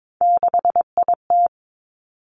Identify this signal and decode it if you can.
Morse code, heard as a keyed tone, about 22 wpm, 700 Hz. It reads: 6IT
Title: Three Challenging Letter-Number Combinations 22wpm